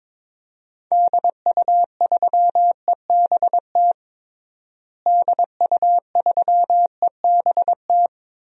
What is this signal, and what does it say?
Morse code, heard as a keyed tone, about 22 wpm, 700 Hz. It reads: DU3EBT DU3EBT
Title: DXCC-100 Call Signs - 6 Characters - Repeated 2x - 22wpm